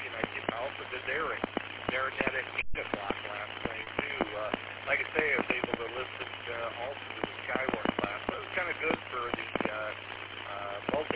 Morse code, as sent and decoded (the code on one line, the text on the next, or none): none